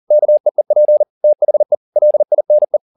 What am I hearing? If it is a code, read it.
Morse code, heard as a keyed tone, about 40 wpm, 600 Hz. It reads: KEEP THE LINE